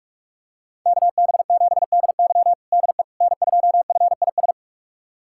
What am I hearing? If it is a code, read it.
Morse code, heard as a keyed tone, about 44 wpm, 700 Hz. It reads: KB7DY DE N2FIS
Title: USA Callsign DE Callsign - 5 Characters - 45wpm